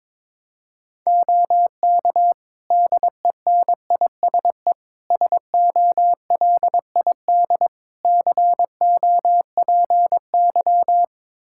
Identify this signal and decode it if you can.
Morse code, heard as a keyed tone, about 22 wpm, 700 Hz. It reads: OK DENISE SOLID COPY
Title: Practice QSO - Exchange 3 - More Info - 22wpm